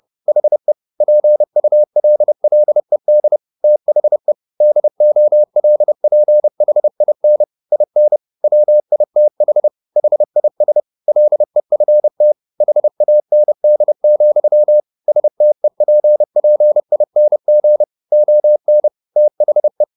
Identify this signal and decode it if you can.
Morse code, heard as a keyed tone, about 30 wpm, 600 Hz. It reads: HE PULLED THE DOLPHIN IN WITH HIS LEFT HAND, STEPPING ON THE